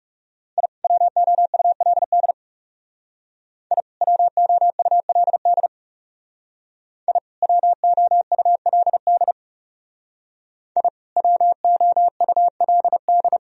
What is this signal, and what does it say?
Morse code, heard as a keyed tone, about 35 wpm, 700 Hz. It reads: I WOULD I WOULD I WOULD I WOULD
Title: Speed-Racing - Sets of 2 Words 30wpm